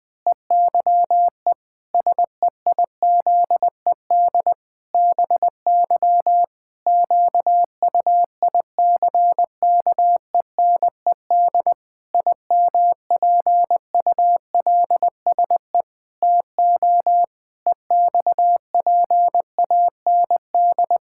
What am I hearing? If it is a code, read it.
Morse code, heard as a keyed tone, about 21 wpm, 700 Hz. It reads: EYE SEIZED BY QUICKENED IMPULSE TO EXPAND
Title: Pangram Challenge v2 - 20wpm